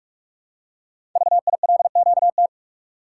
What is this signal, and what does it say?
Morse code, heard as a keyed tone, about 44 wpm, 700 Hz. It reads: VILQT
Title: Code Groups - 5 Letters - 45wpm